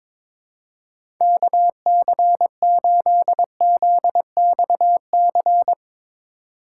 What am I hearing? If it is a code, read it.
Morse code, heard as a keyed tone, about 22 wpm, 700 Hz. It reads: KC8ZXC
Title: USA Call Signs - 6 Characters - 4x Character Spacing - 22wpm